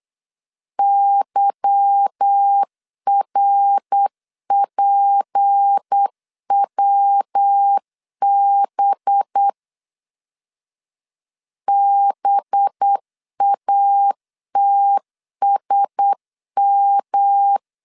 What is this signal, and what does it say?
Morse code, heard as a keyed tone, about 8 wpm, 800 Hz. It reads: YRPWB BATSM
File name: LEZIONI / 8 YJ45 / 8 Y 8wpm 3245